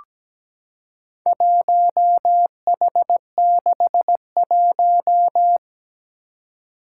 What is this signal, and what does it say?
Morse code, heard as a keyed tone, about 17 wpm, 700 Hz. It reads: 1H61
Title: Four Challenging Letter-Number Combinations 17wpm